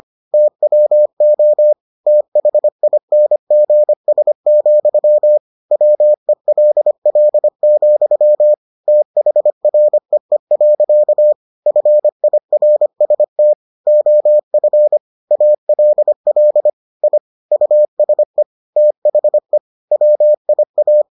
Morse code, heard as a keyed tone, about 25 wpm, 600 Hz. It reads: TWO THINGS, WELL, THREE. FIRST OF ALL I USE THE WIA